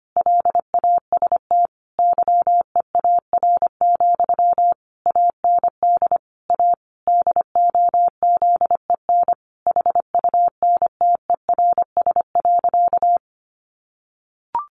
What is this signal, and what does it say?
Morse code, heard as a keyed tone, about 25 wpm, 700 Hz. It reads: LAST YEAR, AND A DOZEN HUNTERS.
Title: A Sound Of Thunder 25wpm